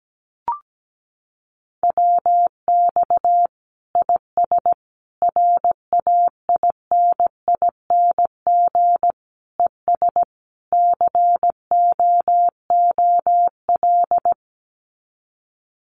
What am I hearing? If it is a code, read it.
Morse code, heard as a keyed tone, about 17 wpm, 700 Hz. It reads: WX IS RAINING ES COOL
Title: Common QSO Phrases 17wpm